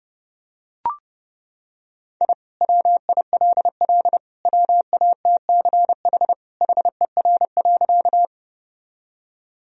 Morse code, heard as a keyed tone, about 30 wpm, 700 Hz. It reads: I WILL WATCH HER.
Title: Sentences from Top 1000 Words - Review Part 1 - 30wpm